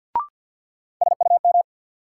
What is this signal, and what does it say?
Morse code, heard as a keyed tone, about 53 wpm, 700 Hz. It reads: SUM
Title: Sentences from Top 1000 Words - Review Part 6 - New Word Drill - 50wpm